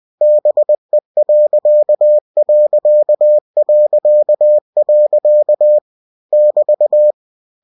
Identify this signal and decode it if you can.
Morse code, heard as a keyed tone, about 20 wpm, 600 Hz. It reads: BE.... <BT>
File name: cw-books / winnie-the-pooh / winnie-20wpm-page-0058